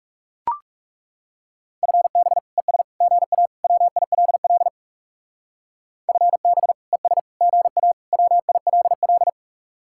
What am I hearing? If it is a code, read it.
Morse code, heard as a keyed tone, about 38 wpm, 700 Hz. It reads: FB ES GA WILL FB ES GA WILL